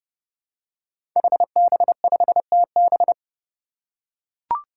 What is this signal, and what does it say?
Morse code, heard as a keyed tone, about 32 wpm, 700 Hz. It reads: HB5TB
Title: Five Challenging Letter-Number Combinations 30wpm